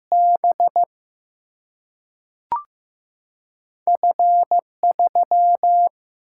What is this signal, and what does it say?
Morse code, heard as a keyed tone, about 15 wpm, 700 Hz. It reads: B F 3
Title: Single-Letter Single-Number 15wpm